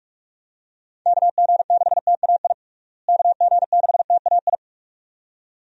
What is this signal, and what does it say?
Morse code, heard as a keyed tone, about 46 wpm, 700 Hz. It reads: KG6TAI KG6TAI
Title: USA Call Signs - 6 Characters - Repeated 2x - 45wpm